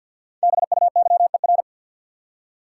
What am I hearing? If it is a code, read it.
Morse code, heard as a keyed tone, about 47 wpm, 700 Hz. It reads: BUYER
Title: Five Letter Words - 50wpm